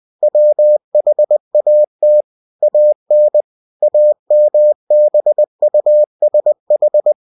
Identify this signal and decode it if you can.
Morse code, heard as a keyed tone, about 20 wpm, 600 Hz. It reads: WHAT AN AMBUSH
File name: cw-books / winnie-the-pooh / winnie-20wpm-page-0082